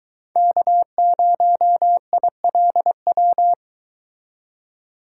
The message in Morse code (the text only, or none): K0ILW